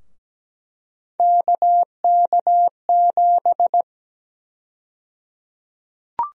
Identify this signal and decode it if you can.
Morse code, heard as a keyed tone, about 17 wpm, 700 Hz. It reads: KK7
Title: USA Call Sign Prefixes - Repeated 2x - 17wpm